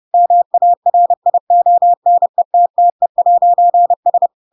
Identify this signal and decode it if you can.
Morse code, heard as a keyed tone, about 30 wpm, 700 Hz. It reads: MARIONETTE'S
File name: Pinocchio / Pinocchio ch3 30 30 ik1hge